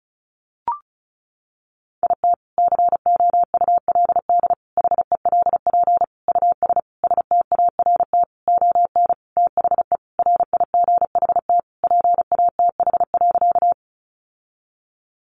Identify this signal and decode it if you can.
Morse code, heard as a keyed tone, about 34 wpm, 700 Hz. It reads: IT COULD HELP US START ON THE RIGHT PATH.
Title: Sentences from Top 1400 Words 35wpm